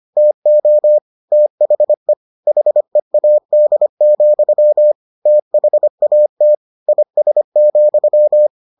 Morse code, heard as a keyed tone, about 25 wpm, 600 Hz. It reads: TO THE HEAD, THAT IS,